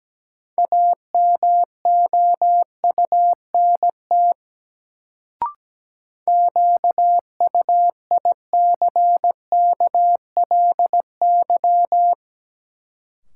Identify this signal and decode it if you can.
Morse code, heard as a keyed tone, about 17 wpm, 700 Hz. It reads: AMOUNT QUICKLY
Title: Sentences from Top 700 Words - New Word Drill - 17wpm